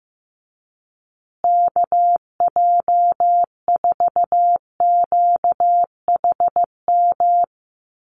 Morse code, heard as a keyed tone, about 15 wpm, 700 Hz. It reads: KJ4QHM
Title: USA Call Signs Repeated 2x 15wpm